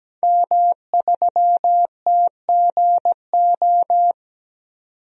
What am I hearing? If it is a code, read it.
Morse code, heard as a keyed tone, about 17 wpm, 700 Hz. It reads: M3TGO
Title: DXCC-100 Call Signs - 5 Characters - 17wpm